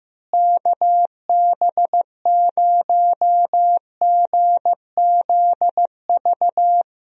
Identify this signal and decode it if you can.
Morse code, heard as a keyed tone, about 16 wpm, 700 Hz. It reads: KB0GZV
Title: USA Call Signs - 6 Characters - Repeated 3x - 15wpm